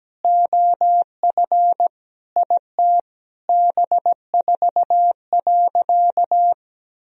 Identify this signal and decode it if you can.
Morse code, heard as a keyed tone, about 17 wpm, 700 Hz. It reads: OF IT B4.
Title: Sentences from Top 400 Words and Abbreviations 17wpm